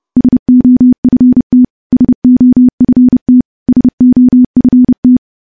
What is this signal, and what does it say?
Morse code, heard as a keyed tone, about 30 wpm, 250 Hz. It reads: SOFT SOFT SOFT
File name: morse / morse-training-30WPM